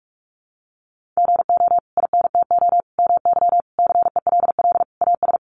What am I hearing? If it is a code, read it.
Morse code, heard as a keyed tone, about 44 wpm, 700 Hz. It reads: GO INTO MY CELL AS
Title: Sentences from Top 1100 Words 45wpm